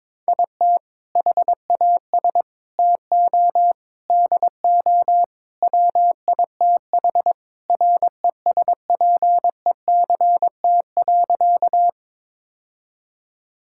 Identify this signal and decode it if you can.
Morse code, heard as a keyed tone, about 22 wpm, 700 Hz. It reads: IT HAS TO DO WITH RESPECT.
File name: practice / 22wpm / Sentences with 4-Word Phrases - 22wpm